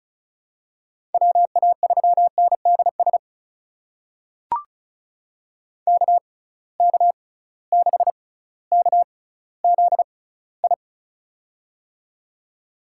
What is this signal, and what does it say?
Morse code, heard as a keyed tone, about 36 wpm, spaced as for 18 wpm, 700 Hz. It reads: WA3NDS KK6KZI
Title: USA Call Signs - 6x Character Spacing - 35wpm